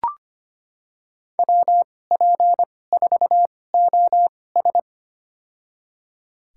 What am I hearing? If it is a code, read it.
Morse code, heard as a keyed tone, about 25 wpm, spaced as for 18 wpm, 700 Hz. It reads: WP4OS
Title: USA Call Signs - 5 Characters - 2x Character Spacing - 25wpm